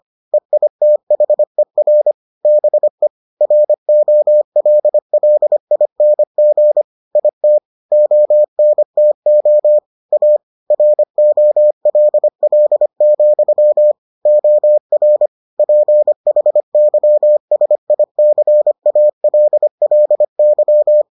Morse code, heard as a keyed tone, about 25 wpm, 600 Hz. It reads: EITHER BE ROLLING IT ONTO A ROLL, OR PHYSICALLY